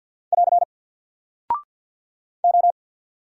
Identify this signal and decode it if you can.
Morse code, heard as a keyed tone, about 50 wpm, 700 Hz. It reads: <AR> O
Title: Single Character (Letter:Number:Punctuation:Prosign) 50wpm